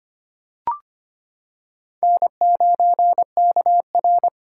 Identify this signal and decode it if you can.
Morse code, heard as a keyed tone, about 25 wpm, 700 Hz. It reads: N9KR
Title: K1USN SST QSO Practice - 25wpm